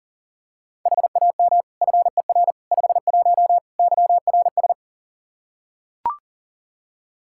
Morse code, heard as a keyed tone, about 40 wpm, 700 Hz. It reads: HAM FER 51 YRS
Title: Practice QSO - Exchange 3 - More Info - 40wpm